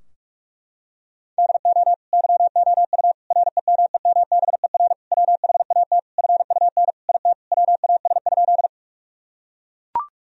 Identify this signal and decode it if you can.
Morse code, heard as a keyed tone, about 44 wpm, 700 Hz. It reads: DO YOU REMEMBER WHAT FUN IT WAS?
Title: Sentences from Top 1000 Words - Review Part 4 - 45wpm